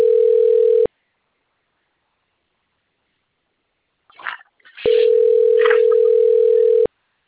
Ringback tone (North America), 440 + 480 Hz.